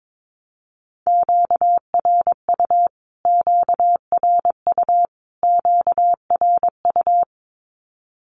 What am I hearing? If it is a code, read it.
Morse code, heard as a keyed tone, about 22 wpm, 700 Hz. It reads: QRU QRU QRU
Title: Q-Codes Repeated 3x 22wpm